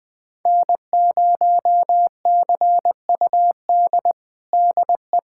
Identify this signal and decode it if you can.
Morse code, heard as a keyed tone, about 21 wpm, 700 Hz. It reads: N0CUD DE
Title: USA Callsign DE Callsign - 5 Characters - 20wpm